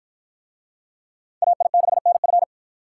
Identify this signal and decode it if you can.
Morse code, heard as a keyed tone, about 48 wpm, 700 Hz. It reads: AI6NF